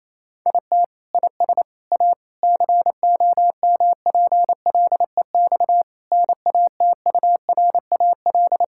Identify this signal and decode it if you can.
Morse code, heard as a keyed tone, about 28 wpm, 700 Hz. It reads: IT IS A COMPLEX NATURAL
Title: Sentences from Top 1000 Words - Review Part 6 - 28wpm